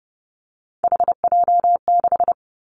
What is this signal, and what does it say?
Morse code, heard as a keyed tone, about 30 wpm, 700 Hz. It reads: HJ6
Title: Callsign Prefixes for Top 100 DXCC - 30wpm